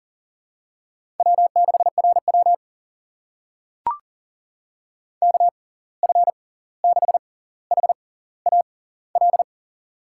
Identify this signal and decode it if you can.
Morse code, heard as a keyed tone, about 40 wpm, spaced as for 18 wpm, 700 Hz. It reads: W6RW KF6HAL
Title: USA Call Signs - 6x Character Spacing - 40wpm